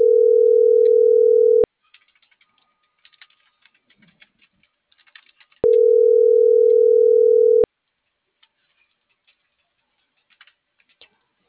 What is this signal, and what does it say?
Ringback tone (North America), 440 + 480 Hz.
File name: CENTRAL NMP / out melanie.diaznmp - .45049